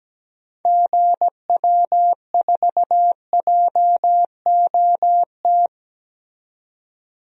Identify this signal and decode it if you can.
Morse code, heard as a keyed tone, about 17 wpm, 700 Hz. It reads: GW4JOT
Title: DXCC-100 Call Signs - 6 Characters - 17wpm